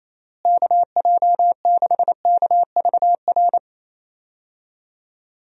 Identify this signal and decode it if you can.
Morse code, heard as a keyed tone, about 28 wpm, 700 Hz. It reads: KJ6KVR
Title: USA Call Signs - 6 Characters - Repeated 2x - 28wpm